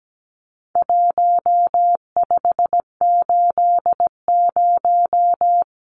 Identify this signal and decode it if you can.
Morse code, heard as a keyed tone, about 17 wpm, 700 Hz. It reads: 1580
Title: Numbers - Contest Style - 4-Digit 17wpm